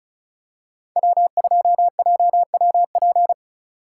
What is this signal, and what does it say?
Morse code, heard as a keyed tone, about 34 wpm, 700 Hz. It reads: W2JWP
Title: USA Call Signs - 5 Characters - 3x Character Spacing - 35wpm